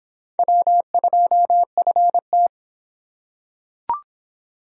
Morse code, heard as a keyed tone, about 26 wpm, 700 Hz. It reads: W2FT